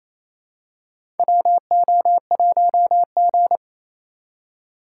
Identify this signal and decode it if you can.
Morse code, heard as a keyed tone, about 28 wpm, 700 Hz. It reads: WO1G
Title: USA Call Signs - 4 Characters - 3x Character Spacing - 28wpm